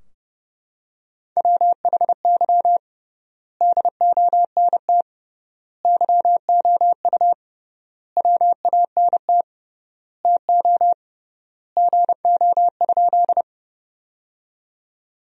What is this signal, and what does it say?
Morse code, heard as a keyed tone, about 30 wpm, 700 Hz. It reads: WHY DONT YOU WANT TO GO?
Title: 3x Word-Spacing - Sentences from Top 300 Words - 30wpm